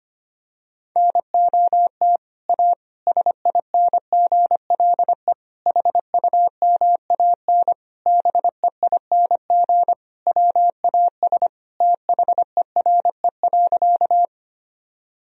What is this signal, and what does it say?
Morse code, heard as a keyed tone, about 25 wpm, 700 Hz. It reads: NOT A SINGLE HUMAN BEING WAS THERE.
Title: Sentences from Top 1000 Words - Review Part 4 - 25wpm